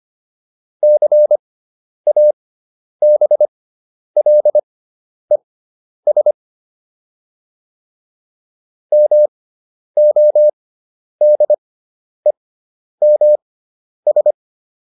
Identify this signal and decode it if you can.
Morse code, heard as a keyed tone, about 25 wpm, spaced as for 10 wpm, 600 Hz. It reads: CABLES MODEMS